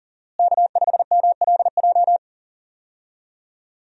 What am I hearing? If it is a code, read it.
Morse code, heard as a keyed tone, about 39 wpm, 700 Hz. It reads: K5MLJ
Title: USA Call Signs - 5 Characters - Repeated 2x - 40wpm